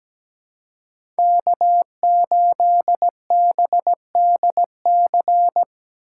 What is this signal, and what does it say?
Morse code, heard as a keyed tone, about 17 wpm, 700 Hz. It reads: K8BDC
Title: USA Call Signs - 5 Characters - 4x Character Spacing - 17wpm